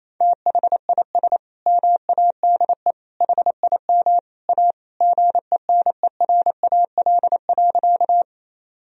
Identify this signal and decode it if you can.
Morse code, heard as a keyed tone, about 28 wpm, 700 Hz. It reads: THIS MADE HIM A GENERAL.
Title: Sentences from Top 1000 Words - Review Part 3 - 28wpm